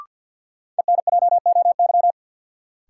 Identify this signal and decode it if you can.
Morse code, heard as a keyed tone, about 49 wpm, 700 Hz. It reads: ENJOY
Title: Sentences from Top 1000 Words - Review Part 4 - New Word Drill - 50wpm